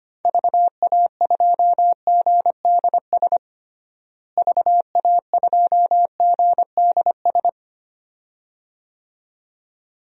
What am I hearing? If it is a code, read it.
Morse code, heard as a keyed tone, about 25 wpm, 700 Hz. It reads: VA2GDS VA2GDS
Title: DXCC-100 Call Signs - 6 Characters - Repeated 2x - 25wpm